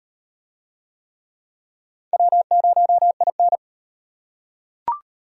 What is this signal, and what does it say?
Morse code, heard as a keyed tone, about 36 wpm, 700 Hz. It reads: W0IN